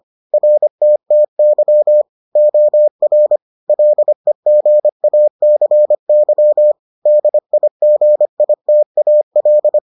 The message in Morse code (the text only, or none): RTTY OR LEGACY DIGITAL